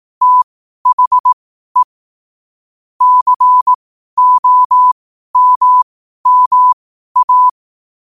Morse code, heard as a keyed tone, about 18 wpm, spaced as for 12 wpm, 1000 Hz. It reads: THE COMMA
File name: qotdfiles / 12wpm